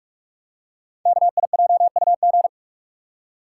Morse code, heard as a keyed tone, about 44 wpm, 690 Hz. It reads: KIJUG
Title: Code Groups - 5 Letters - 45wpm